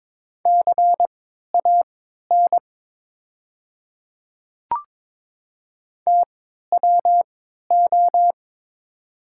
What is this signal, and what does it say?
Morse code, heard as a keyed tone, about 22 wpm, spaced as for 12 wpm, 700 Hz. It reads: CAN TWO
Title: 3x Character Spacing - Top 100 Words - 22wpm